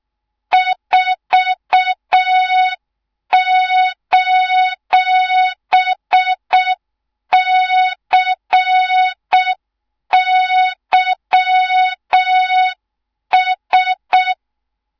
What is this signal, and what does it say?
Morse code, heard as a keyed tone, about 6 wpm, 3700 Hz. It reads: 4:CYS
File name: Morze-HA2MN / hang / 24 GYAKORLAT